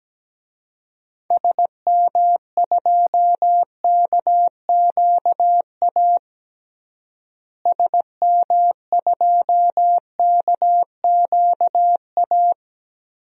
Morse code, heard as a keyed tone, about 17 wpm, 700 Hz. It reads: SM2KQA SM2KQA
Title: DXCC-100 Call Signs - 6 Characters - Repeated 3x - 17wpm